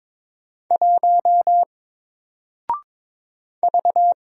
Morse code, heard as a keyed tone, about 22 wpm, 700 Hz. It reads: 1 V
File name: Single-Letter-Number-22wpm